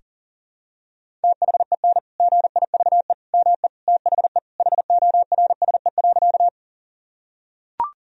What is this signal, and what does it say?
Morse code, heard as a keyed tone, about 37 wpm, 700 Hz. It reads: THEN GIVE ME THE HORSE.
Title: Sentences from Top 1000 Words - Review Part 1 - 40wpm